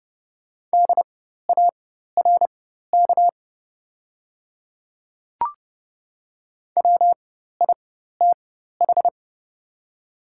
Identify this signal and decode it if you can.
Morse code, heard as a keyed tone, about 30 wpm, spaced as for 14 wpm, 690 Hz. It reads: DARK WITH
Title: 4x Character Spacing - Top 500 Words - 30wpm